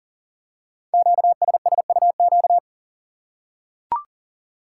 Morse code, heard as a keyed tone, about 43 wpm, 700 Hz. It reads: QSSUQ
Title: Code Groups - 5 Letters - 40wpm